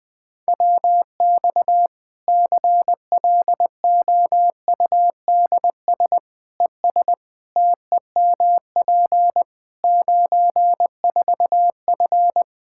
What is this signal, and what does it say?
Morse code, heard as a keyed tone, about 21 wpm, 700 Hz. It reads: WX CLOUDS ES TEMP 94F
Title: K1USN SST QSO Practice 20wpm